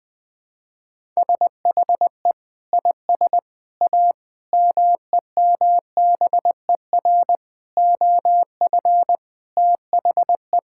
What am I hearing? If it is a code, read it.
Morse code, heard as a keyed tone, about 20 wpm, 700 Hz. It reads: SHE IS A MEMBER OF THE
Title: Sentences with 4-Word Phrases - 20wpm